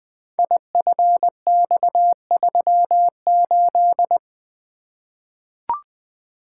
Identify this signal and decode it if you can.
Morse code, heard as a keyed tone, about 20 wpm, 700 Hz. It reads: IFX38
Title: Code Groups - 5 Letter:Numbers - 20wpm